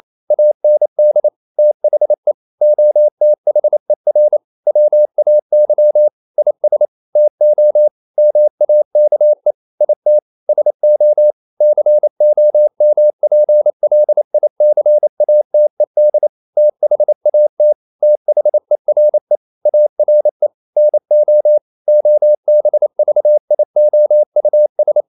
Morse code, heard as a keyed tone, about 28 wpm, 600 Hz. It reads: AND THE OTHER WAY IS TO MAKE IT SO COMPLICATED THAT THERE ARE NO OBVIOUS
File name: CW / Practice / EnglishQuotations0228